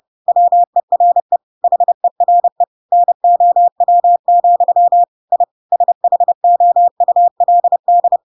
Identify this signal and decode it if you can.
Morse code, heard as a keyed tone, about 30 wpm, 700 Hz. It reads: WERE HERE NOW, I SHOULD
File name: Pinocchio / Pinocchio ch5 30 30 ik1hge